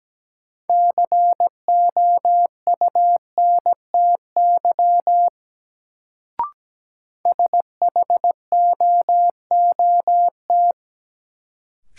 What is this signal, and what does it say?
Morse code, heard as a keyed tone, about 17 wpm, 700 Hz. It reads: COUNTY SHOOT
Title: Sentences from Top 1200 Words - New Word Drill - 17wpm